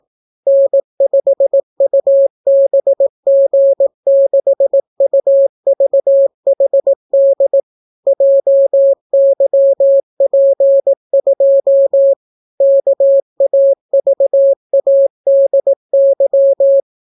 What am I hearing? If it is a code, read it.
Morse code, heard as a keyed tone, about 18 wpm, 550 Hz. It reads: N5UBG6UVHD JYP2 KAVADY